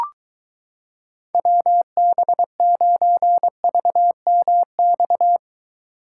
Morse code, heard as a keyed tone, about 24 wpm, 700 Hz. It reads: WB9VMX